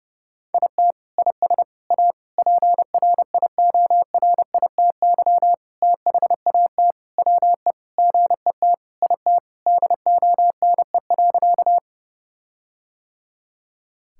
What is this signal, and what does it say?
Morse code, heard as a keyed tone, about 30 wpm, 700 Hz. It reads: IT IS A PRIORITY THAT WE GET IT DONE.
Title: Sentences from Top 1600 Words 30wpm